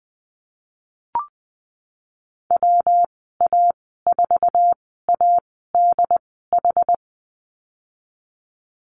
Morse code, heard as a keyed tone, about 20 wpm, spaced as for 15 wpm, 700 Hz. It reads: WA4ADH
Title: 140 Lesson 23g - Letter f - Call Signs 20wpm